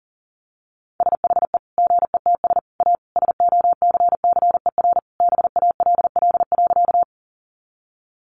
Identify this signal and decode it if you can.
Morse code, heard as a keyed tone, about 40 wpm, 700 Hz. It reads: SHE GETS A SOCCER BALL.
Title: Easy Phrases 40wpm